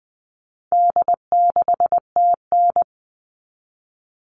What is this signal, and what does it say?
Morse code, heard as a keyed tone, about 20 wpm, 700 Hz. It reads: D6TN
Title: All Cut Numbers - 4-Digit 20wpm